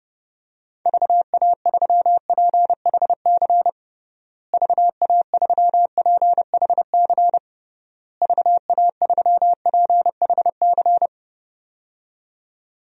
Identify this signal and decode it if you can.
Morse code, heard as a keyed tone, about 30 wpm, 700 Hz. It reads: VA3PHC VA3PHC VA3PHC
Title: DXCC-100 Call Signs - 6 Characters - Repeated 3x - 30wpm